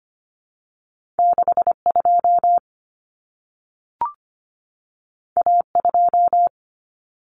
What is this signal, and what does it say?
Morse code, heard as a keyed tone, about 25 wpm, 700 Hz. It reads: 62 A2
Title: Common Cut Numbers - 2-Digit 25wpm